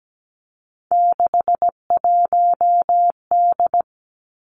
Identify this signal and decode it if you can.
Morse code, heard as a keyed tone, about 17 wpm, 700 Hz. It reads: 61D
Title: Three Challenging Letter-Number Combinations 17wpm